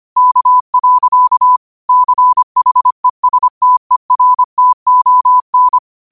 Morse code, heard as a keyed tone, about 25 wpm, 1000 Hz. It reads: K. CHESTERTON
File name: qotdfiles / 25wpm